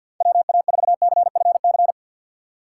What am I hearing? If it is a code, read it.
Morse code, heard as a keyed tone, about 52 wpm, 700 Hz. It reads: WA4CFC
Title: USA Call Signs - 6 Characters - 6x Character Spacing - 50wpm